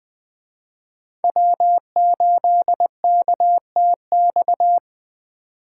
Morse code, heard as a keyed tone, about 20 wpm, 700 Hz. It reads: W8KTX
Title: Speed Racing - USA 1x3 Call Signs 17wpm